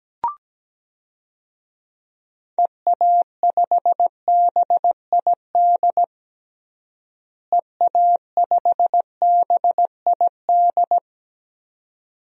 Morse code, heard as a keyed tone, about 17 wpm, 700 Hz. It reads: EA5BID EA5BID
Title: DXCC-100 Call Signs - 6 Characters - Repeated 2x - 17wpm